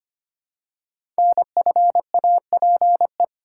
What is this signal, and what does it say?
Morse code, heard as a keyed tone, about 25 wpm, 700 Hz. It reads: NFAPE
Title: Code Groups - 5 Letters - 25wpm